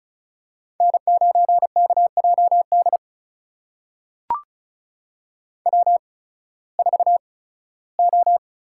Morse code, heard as a keyed tone, about 36 wpm, spaced as for 19 wpm, 700 Hz. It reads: N9KJD W4O
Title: USA Call Signs - 5 Characters - 8x Character Spacing - 35wpm